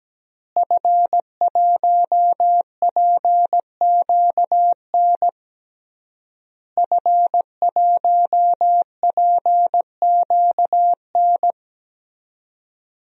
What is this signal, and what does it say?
Morse code, heard as a keyed tone, about 17 wpm, 700 Hz. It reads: F1PQN F1PQN
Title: DXCC-100 Call Signs - 5 Characters - Repeated 3x - 17wpm